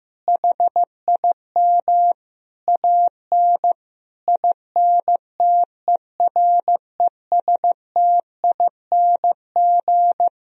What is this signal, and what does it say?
Morse code, heard as a keyed tone, about 15 wpm, 700 Hz. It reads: HIM AN INTERESTING
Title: Sentences from Top 1000 Words - Review Part 1 - 15wpm